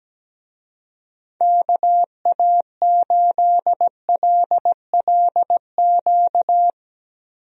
Morse code, heard as a keyed tone, about 17 wpm, 700 Hz. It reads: KA8LLQ
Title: USA Call Signs - 6 Characters - Repeated 3x - 17wpm